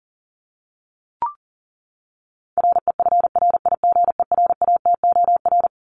Morse code, heard as a keyed tone, about 40 wpm, 700 Hz. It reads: REFRIGERATOR
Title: Top 1501-1750 Words 40wpm